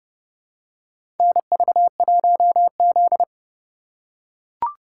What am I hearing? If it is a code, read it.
Morse code, heard as a keyed tone, about 30 wpm, 700 Hz. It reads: NV1Z
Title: USA Call Signs - 4 Characters - 3x Character Spacing - 30wpm